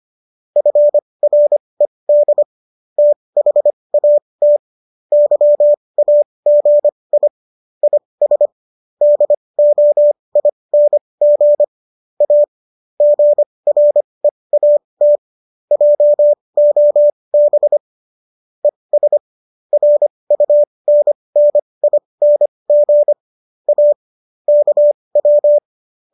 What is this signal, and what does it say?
Morse code, heard as a keyed tone, about 25 wpm, spaced as for 20 wpm, 590 Hz. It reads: FRED THAT YAGI IS DOING A GREAT JOB ES RUNNING A KW